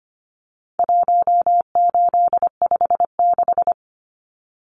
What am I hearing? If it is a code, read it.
Morse code, heard as a keyed tone, about 25 wpm, 700 Hz. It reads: 1856
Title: Numbers - 4-Digit 25wpm